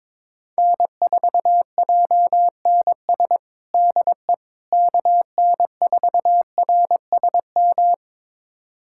Morse code, heard as a keyed tone, about 22 wpm, 700 Hz. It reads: N4JNS DE KN4RSM
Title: USA Callsign DE Callsign 22wpm